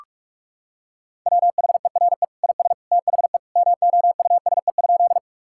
Morse code, heard as a keyed tone, about 46 wpm, 700 Hz. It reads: WHERE IS THE MOUSE?
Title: Sentences from Top 1000 Words - Review Part 5 - 45wpm